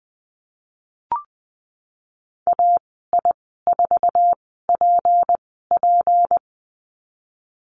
Morse code, heard as a keyed tone, about 20 wpm, spaced as for 15 wpm, 700 Hz. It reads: AI4PP
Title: 179 Lesson 29g - Slash - Call Signs 20wpm